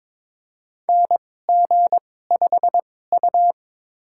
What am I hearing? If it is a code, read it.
Morse code, heard as a keyed tone, about 22 wpm, spaced as for 16 wpm, 700 Hz. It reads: NG5U
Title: USA Call Signs - 4 Characters - 2x Character Spacing - 22wpm